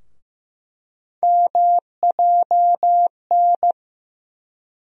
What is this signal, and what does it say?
Morse code, heard as a keyed tone, about 15 wpm, 700 Hz. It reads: MJN
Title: USA Call Sign Postfixes - 3 Letters 15wpm